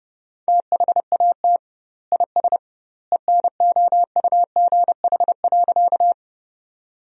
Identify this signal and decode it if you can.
Morse code, heard as a keyed tone, about 30 wpm, 700 Hz. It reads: THAT IS ENOUGH.
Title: 2x Word-Spacing - Sentences from Top 400 Words - 30wpm